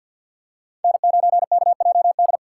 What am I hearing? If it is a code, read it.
Morse code, heard as a keyed tone, about 49 wpm, 690 Hz. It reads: N9KJD
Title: USA Call Signs - 4x Character Spacing - 50wpm